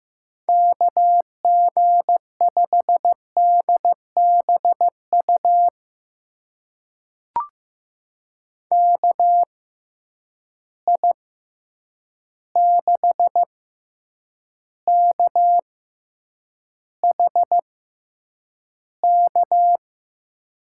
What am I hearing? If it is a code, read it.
Morse code, heard as a keyed tone, about 15 wpm, spaced as for 8 wpm, 700 Hz. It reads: KG5DBU KI6KHK
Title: USA Call Signs - 6 Characters - 6x Character Spacing - 15wpm